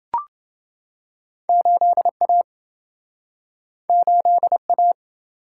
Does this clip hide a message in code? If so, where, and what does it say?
Yes: Morse code from 1.5 to 4.9 s, heard as a keyed tone, about 28 wpm, 700 Hz. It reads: 8A 8A